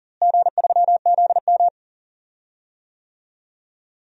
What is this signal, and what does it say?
Morse code, heard as a keyed tone, about 40 wpm, 700 Hz. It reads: G3ZM
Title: DXCC-100 Call Signs - 4 Characters - Repeated 3x - 40wpm